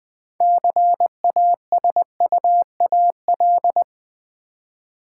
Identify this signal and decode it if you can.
Morse code, heard as a keyed tone, about 20 wpm, 700 Hz. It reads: CASUAL
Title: 6 Letter Words - First 3 Letters Given - 20wpm